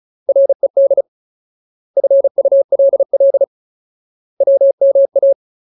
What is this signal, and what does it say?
Morse code, heard as a keyed tone, about 34 wpm, 550 Hz. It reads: RED FULL WMA
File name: Ultimate Word List / 35WPM Ultimate